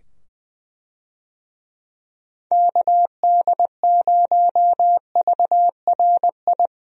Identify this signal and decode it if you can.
Morse code, heard as a keyed tone, about 20 wpm, 700 Hz. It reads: KD0VRI
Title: USA Call Signs - 6 Characters - 20wpm